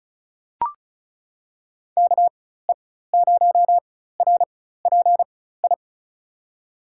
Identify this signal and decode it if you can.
Morse code, heard as a keyed tone, about 36 wpm, spaced as for 16 wpm, 700 Hz. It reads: KE0RPI
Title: USA Call Signs - 4x Character Spacing - 35wpm